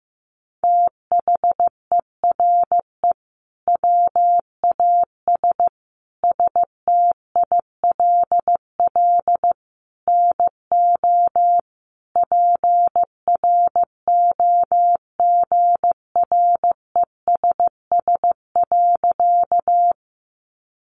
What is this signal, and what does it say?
Morse code, heard as a keyed tone, about 15 wpm, 700 Hz. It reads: THERE WAS STILL NO PROGRESS.
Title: Sentences from Top 1400 Words 15wpm